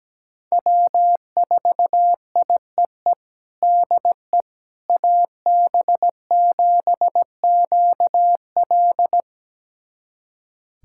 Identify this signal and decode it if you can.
Morse code, heard as a keyed tone, about 17 wpm, 700 Hz. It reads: W4IEE DE AB7QL
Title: USA Callsign DE Callsign - 5 Characters - 17wpm